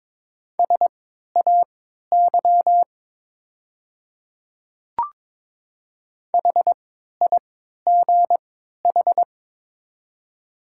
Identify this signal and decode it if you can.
Morse code, heard as a keyed tone, about 22 wpm, spaced as for 12 wpm, 690 Hz. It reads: SAY HIGH
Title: 3x Character Spacing - Top 300 Words - 22wpm